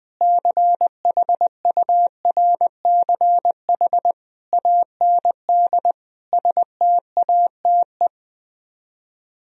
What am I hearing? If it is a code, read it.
Morse code, heard as a keyed tone, about 20 wpm, 700 Hz. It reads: CHURCH AND STATE
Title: Speed-Racing - Sets of 5 Words 20wpm